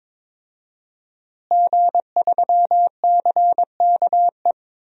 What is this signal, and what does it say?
Morse code, heard as a keyed tone, about 22 wpm, 700 Hz. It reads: G3CKE
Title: DXCC-100 Call Signs - 5 Characters - Repeated 2x - 22wpm